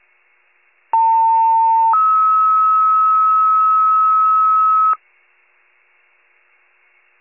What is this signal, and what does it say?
Two-tone page, heard as a one-second tone, then a three-second tone, about 900 then 1290 Hz. It is heard from 0.9 to 4.9 s.